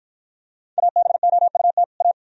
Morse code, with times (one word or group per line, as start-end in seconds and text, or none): 0.8-1.8 ABOUT
2.0-2.1 A